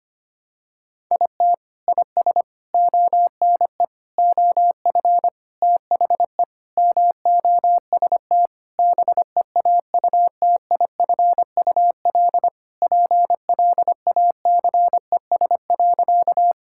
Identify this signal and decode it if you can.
Morse code, heard as a keyed tone, about 25 wpm, 700 Hz. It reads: IT IS ONE OF THE MOST BEAUTIFUL PLACES.
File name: practice / 25wpm / Sentences with 4-Word Phrases - 25wpm